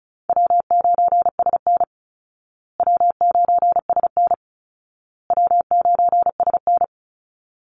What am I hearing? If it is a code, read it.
Morse code, heard as a keyed tone, about 34 wpm, 700 Hz. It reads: W9SN W9SN W9SN
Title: Easy Call Signs Repeated 3x - 35wpm